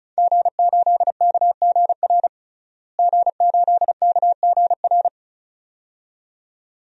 Morse code, heard as a keyed tone, about 34 wpm, 700 Hz. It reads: G8KGR G8KGR
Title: DXCC-100 Call Signs - 5 Characters - Repeated 2x - 35wpm